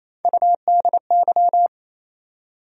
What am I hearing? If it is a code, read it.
Morse code, heard as a keyed tone, about 28 wpm, 700 Hz. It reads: UDY
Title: USA Call Sign Postfixes - 3 Letters 28wpm